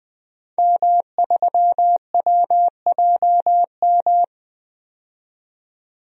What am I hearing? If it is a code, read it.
Morse code, heard as a keyed tone, about 20 wpm, 700 Hz. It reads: M3WJM
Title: DXCC-100 Call Signs - 5 Characters - 20wpm